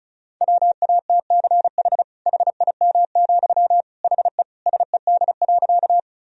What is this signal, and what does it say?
Morse code, heard as a keyed tone, about 36 wpm, 700 Hz. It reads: WATCH HIM, HE SED.
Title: Sentences from Top 300 Words and Abbreviations 35wpm